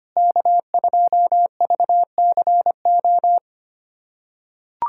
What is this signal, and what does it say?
Morse code, heard as a keyed tone, about 25 wpm, 700 Hz. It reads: K2VCO
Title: USA Callsign DE Callsign - 5 Characters - 25wpm